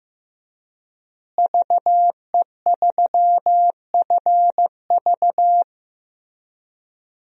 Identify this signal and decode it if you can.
Morse code, heard as a keyed tone, about 15 wpm, 700 Hz. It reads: VE3FV
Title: DXCC-100 Call Signs - 5 Characters - Repeated 3x - 15wpm